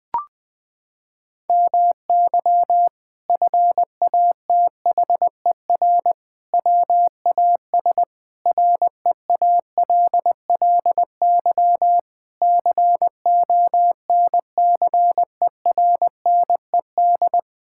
Morse code, heard as a keyed tone, about 19 wpm, 700 Hz. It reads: MY FATHER WAS REALLY CONCERNED
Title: Sentences from Top 1000 Words - Review Part 6 - 20wpm